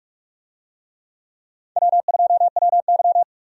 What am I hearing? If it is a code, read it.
Morse code, heard as a keyed tone, about 41 wpm, 700 Hz. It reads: W2WY
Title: USA Call Signs - 4 Characters - Repeated 3x - 45wpm